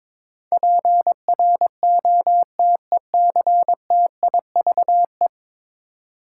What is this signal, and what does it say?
Morse code, heard as a keyed tone, about 21 wpm, 700 Hz. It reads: PROTECTIVE
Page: morse-code-ninja-practice-22wpm